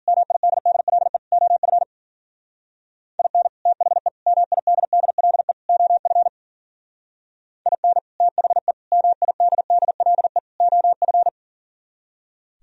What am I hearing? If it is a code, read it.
Morse code, heard as a keyed tone, about 52 wpm, 700 Hz. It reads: MIDDLE OF IN THE MIDDLE OF IN THE MIDDLE OF